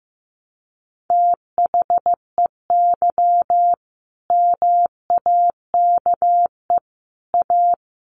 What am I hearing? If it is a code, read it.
Morse code, heard as a keyed tone, about 15 wpm, 700 Hz. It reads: THEY MAKE A
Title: Sentences from Top 1200 Words 15wpm